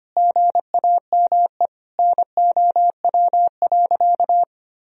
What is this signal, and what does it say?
Morse code, heard as a keyed tone, about 25 wpm, 700 Hz. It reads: GAME NOW.
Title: Sentences from Top 1000 Words - Review Part 3 - 25wpm